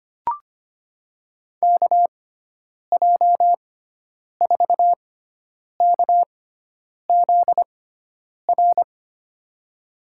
Morse code, heard as a keyed tone, about 25 wpm, spaced as for 9 wpm, 700 Hz. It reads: KJ4KZR
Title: USA Call Signs - 6 Characters - 6x Character Spacing - 25wpm